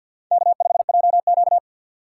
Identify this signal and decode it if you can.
Morse code, heard as a keyed tone, about 48 wpm, 700 Hz. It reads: K5JQ
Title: USA Call Signs - 4 Characters - 2x Character Spacing - 50wpm